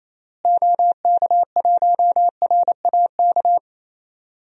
Morse code, heard as a keyed tone, about 27 wpm, 700 Hz. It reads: OK1RAK
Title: DXCC-100 Call Signs - 6 Characters - Repeated 3x - 28wpm